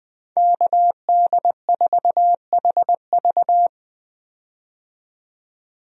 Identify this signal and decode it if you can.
Morse code, heard as a keyed tone, about 20 wpm, 700 Hz. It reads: KD4HV
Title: USA Call Signs - 5 Characters - 20wpm